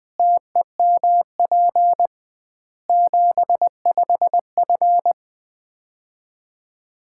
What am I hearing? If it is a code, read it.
Morse code, heard as a keyed tone, about 20 wpm, 700 Hz. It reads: TEMP 75F
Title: Common QSO Phrases - 2x Word-Spacing - 20wpm